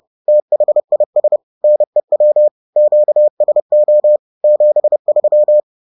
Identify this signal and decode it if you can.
Morse code, heard as a keyed tone, about 30 wpm, 600 Hz. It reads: THIS NEW QSO 73